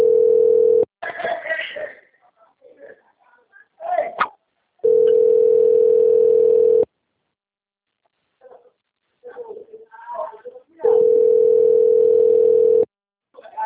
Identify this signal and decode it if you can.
Ringback tone (North America), 440 + 480 Hz, with speech.
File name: out- - - - - .29986